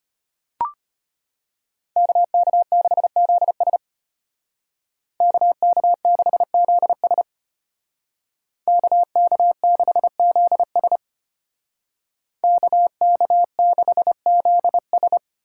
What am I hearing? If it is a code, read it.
Morse code, heard as a keyed tone, about 33 wpm, 700 Hz. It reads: KK6ZS KK6ZS KK6ZS KK6ZS